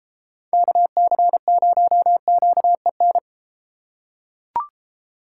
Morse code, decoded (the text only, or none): KC0QEN